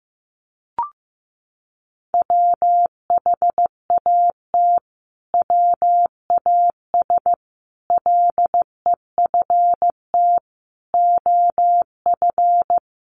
Morse code, heard as a keyed tone, about 15 wpm, 700 Hz. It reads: WHAT WAS LEFT OF
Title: Sentences from Top 300 Words 15wpm